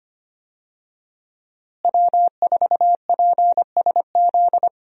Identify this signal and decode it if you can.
Morse code, heard as a keyed tone, about 25 wpm, 700 Hz. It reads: W4PSZ
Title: USA Call Signs - 5 Characters - Repeated 3x - 25wpm